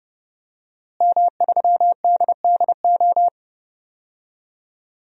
Morse code, heard as a keyed tone, about 31 wpm, 700 Hz. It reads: M3DDO
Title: DXCC-100 Call Signs - 5 Characters - Repeated 3x - 30wpm